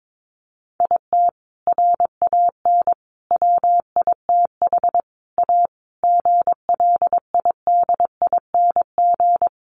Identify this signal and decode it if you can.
Morse code, heard as a keyed tone, about 22 wpm, 700 Hz. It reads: IT RAN WITH A GLIDING
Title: A Sound Of Thunder 22wpm